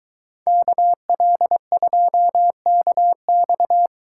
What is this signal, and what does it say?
Morse code, heard as a keyed tone, about 22 wpm, 700 Hz. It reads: KL2KX